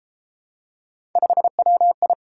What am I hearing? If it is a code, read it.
Morse code, heard as a keyed tone, about 32 wpm, 700 Hz. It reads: 5WI